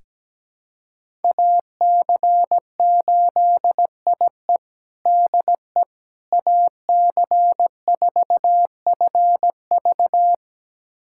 Morse code, heard as a keyed tone, about 17 wpm, 700 Hz. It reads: AC8IE DE AC4FV
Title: USA Callsign DE Callsign - 5 Characters - 17wpm